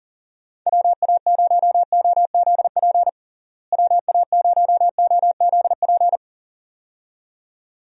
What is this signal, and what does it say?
Morse code, heard as a keyed tone, about 40 wpm, 700 Hz. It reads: WA0OZP WA0OZP
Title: USA Call Signs - 6 Characters - Repeated 2x - 40wpm